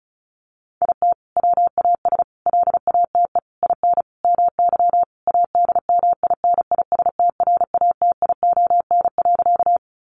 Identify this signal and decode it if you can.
Morse code, heard as a keyed tone, about 34 wpm, 700 Hz. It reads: IT WAS LATE IN MY ADMINISTRATION.
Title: Sentences from Top 1300 Words 35wpm